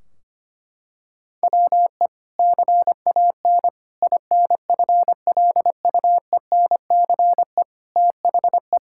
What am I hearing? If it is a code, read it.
Morse code, heard as a keyed tone, about 25 wpm, 700 Hz. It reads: WE CAN INFLUENCE THE
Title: Sentences from Top 1600 Words 25wpm